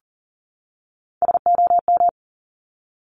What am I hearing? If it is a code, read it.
Morse code, heard as a keyed tone, about 38 wpm, 690 Hz. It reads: SOM
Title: Top Three Letter Combinations in Words 40wpm